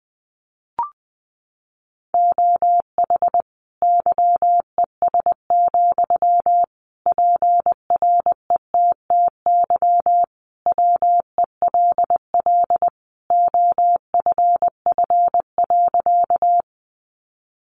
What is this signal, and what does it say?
Morse code, heard as a keyed tone, about 20 wpm, 700 Hz. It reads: OH YES, PRETTY WELL OFF.
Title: Sentences from Top 1000 Words 20wpm